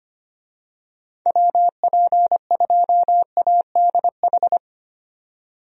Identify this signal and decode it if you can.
Morse code, heard as a keyed tone, about 25 wpm, 700 Hz. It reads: WP2ADH
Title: USA Call Signs - 6 Characters - 25wpm